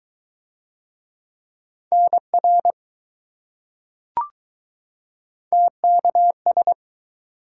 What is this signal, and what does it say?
Morse code, heard as a keyed tone, about 20 wpm, 700 Hz. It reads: NR TKS